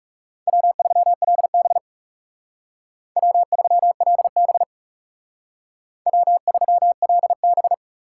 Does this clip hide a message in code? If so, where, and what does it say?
Yes: Morse code from 0.5 to 7.7 s, heard as a keyed tone, about 38 wpm, 700 Hz. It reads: W3LB W3LB W3LB